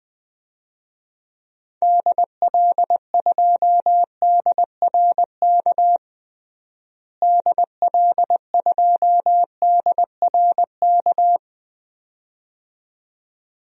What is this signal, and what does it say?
Morse code, heard as a keyed tone, about 20 wpm, 700 Hz. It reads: DL2DRK DL2DRK
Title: DXCC-100 Call Signs - 6 Characters - Repeated 2x - 20wpm